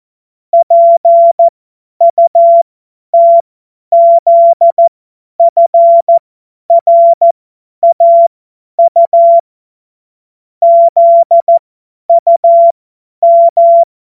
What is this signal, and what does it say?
Morse code, heard as a keyed tone, about 14 wpm, spaced as for 10 wpm, 680 Hz. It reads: PUTZFRAU ZUM
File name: Morsekurs / Witze / Witz 13 Mai 2025 50 F70